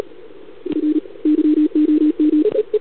Morse code, heard as a keyed tone, about 44 wpm, 300 Hz. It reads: F YOM
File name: CQWWCW2021 / UN7LDR